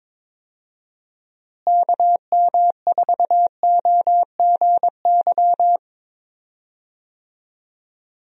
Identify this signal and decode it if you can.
Morse code, heard as a keyed tone, about 22 wpm, 700 Hz. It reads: KM4OGY
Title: USA Call Signs - 6 Characters - Repeated 3x - 22wpm